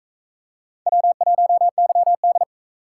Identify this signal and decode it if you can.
Morse code, heard as a keyed tone, about 42 wpm, 700 Hz. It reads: W1YD